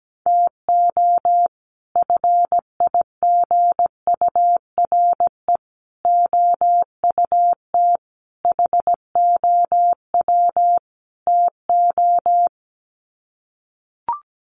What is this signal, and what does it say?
Morse code, heard as a keyed tone, about 17 wpm, 700 Hz. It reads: TO FIGURE OUT HOW TO
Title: Sets of 5 Words 17wpm